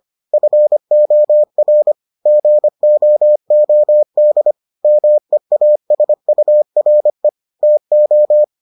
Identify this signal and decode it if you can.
Morse code, heard as a keyed tone, about 25 wpm, 600 Hz. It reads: FOR GOOD MEASURE TO